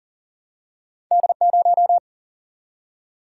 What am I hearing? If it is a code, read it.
Morse code, heard as a keyed tone, about 40 wpm, 700 Hz. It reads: D0
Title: Single-Letter Single-Number 40wpm